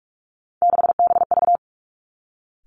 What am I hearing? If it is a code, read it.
Morse code, heard as a keyed tone, about 46 wpm, 700 Hz. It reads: 6BV
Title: Three Challenging Letter-Number Combinations 45wpm